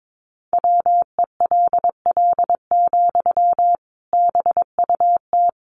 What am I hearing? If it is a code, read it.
Morse code, heard as a keyed tone, about 22 wpm, 700 Hz. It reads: WELL, BUT
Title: Sentences from Top 700 Words 22wpm